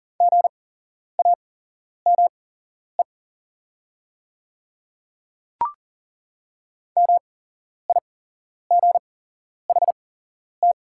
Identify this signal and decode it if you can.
Morse code, heard as a keyed tone, about 41 wpm, spaced as for 11 wpm, 690 Hz. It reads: GAME MIGHT
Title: 8x Character Spacing - Top 500 Words - 40wpm